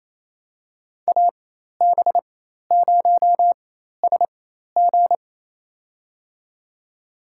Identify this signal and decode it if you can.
Morse code, heard as a keyed tone, about 28 wpm, spaced as for 13 wpm, 700 Hz. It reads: AB0SG
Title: USA Call Signs - 5 Characters - 4x Character Spacing - 28wpm